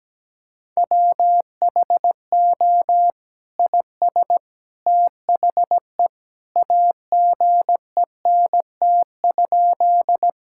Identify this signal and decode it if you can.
Morse code, heard as a keyed tone, about 17 wpm, 700 Hz. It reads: WHO IS THE AGENT?
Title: Sentences from Top 1600 Words 17wpm